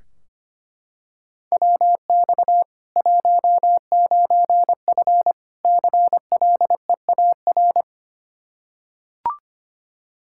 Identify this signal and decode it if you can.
Morse code, heard as a keyed tone, about 25 wpm, 700 Hz. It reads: WX 19F CLEAR
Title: Practice QSO - Exchange 2 - Info - 25wpm